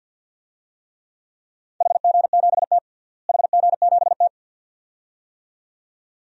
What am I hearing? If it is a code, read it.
Morse code, heard as a keyed tone, about 49 wpm, 700 Hz. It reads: HG7T HG7T
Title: DXCC-100 Call Signs - 4 Characters - Repeated 2x - 50wpm